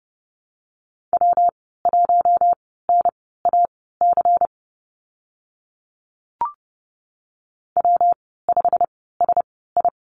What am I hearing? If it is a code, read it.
Morse code, heard as a keyed tone, about 30 wpm, spaced as for 17 wpm, 700 Hz. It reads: W1NAC W5SI
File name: Lesson 19g - Letter w - Call Signs 30wpm